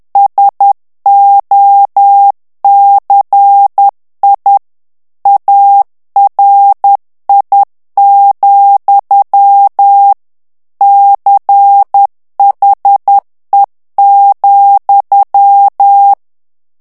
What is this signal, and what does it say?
Morse code, heard as a keyed tone, about 10 wpm, 800 Hz. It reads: SOCI ARI, CHE,
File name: ITALIEN texte clair 2 vitesse 10 mots minute